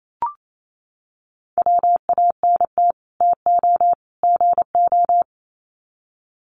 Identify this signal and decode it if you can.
Morse code, heard as a keyed tone, about 28 wpm, 700 Hz. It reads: WANT TO GO
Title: Sets of 3 Words 28wpm